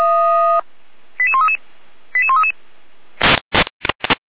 Five-tone selective call (ZVEI), heard as quick runs of five tones, tones of about 70 ms. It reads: 88133 88133